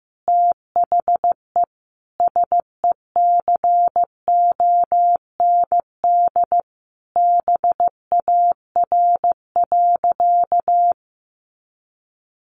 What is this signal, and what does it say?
Morse code, heard as a keyed tone, about 15 wpm, 700 Hz. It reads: THE SECOND BAR.
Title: Easy Phrases 15wpm